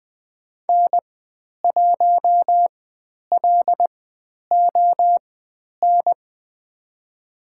Morse code, heard as a keyed tone, about 20 wpm, spaced as for 10 wpm, 700 Hz. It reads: N1LON
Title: 095 Lesson 16g - Letter c - Call Signs 20wpm